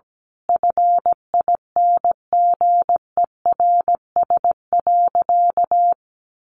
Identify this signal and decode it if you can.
Morse code, heard as a keyed tone, about 17 wpm, 700 Hz. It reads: FINGERS.
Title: A Sound Of Thunder 17wpm